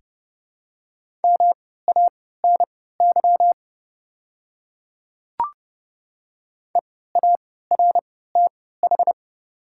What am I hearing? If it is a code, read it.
Morse code, heard as a keyed tone, about 30 wpm, spaced as for 17 wpm, 700 Hz. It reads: MANY EARTH
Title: 3x Character Spacing - Top 200 Words - 30wpm